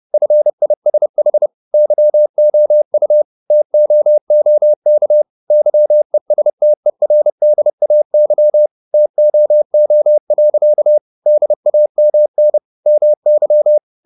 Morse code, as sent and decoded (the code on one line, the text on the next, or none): ..-. .. ... .... / -.-- --- ..- / - --- --- -.- / -.-- . ... - . .-. -.. .- -.-- / - --- --- .-.-.- / -.. .- -- -. / -- -.--
FISH YOU TOOK YESTERDAY TOO. DAMN MY